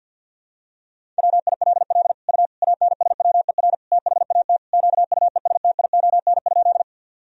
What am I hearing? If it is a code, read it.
Morse code, heard as a keyed tone, about 46 wpm, 700 Hz. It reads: WILL U ANSWER THAT QUESTION?
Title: Sentences from Top 400 Words and Abbreviations 50wpm